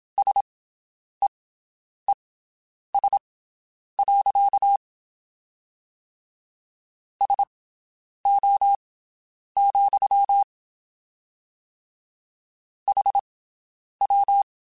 Morse code, heard as a keyed tone, about 27 wpm, spaced as for 9 wpm, 800 Hz. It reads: SEES. SO, HW
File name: QSOs Spaced